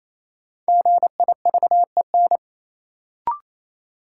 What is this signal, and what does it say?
Morse code, heard as a keyed tone, about 28 wpm, 700 Hz. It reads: GIVEN
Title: Sentences from Top 1400 Words - New Word Drill - 28wpm